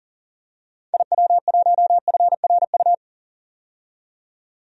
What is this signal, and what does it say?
Morse code, heard as a keyed tone, about 40 wpm, 700 Hz. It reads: IW1FRU
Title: DXCC-100 Call Signs - 6 Characters - 40wpm